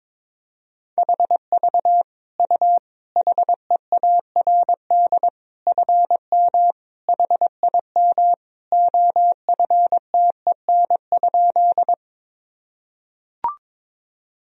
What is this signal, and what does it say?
Morse code, heard as a keyed tone, about 22 wpm, 700 Hz. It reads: HV U HEARD FM HIM OFTEN?
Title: Sentences from Top 400 Words and Abbreviations 22wpm